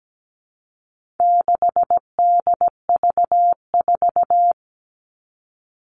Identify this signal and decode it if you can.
Morse code, heard as a keyed tone, about 17 wpm, 700 Hz. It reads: 6DV4
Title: All Cut Numbers - 4-Digit 17wpm